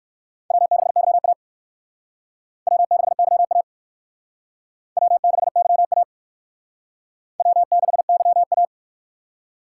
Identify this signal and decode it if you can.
Morse code, heard as a keyed tone, about 56 wpm, 700 Hz. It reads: W6YA W6YA W6YA W6YA
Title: Speed Racing - USA 1x2 Call Signs 45wpm